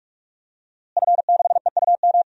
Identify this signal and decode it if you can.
Morse code, heard as a keyed tone, about 42 wpm, 700 Hz. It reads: F6EUM
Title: DXCC-100 Call Signs - 5 Characters - Repeated 2x - 45wpm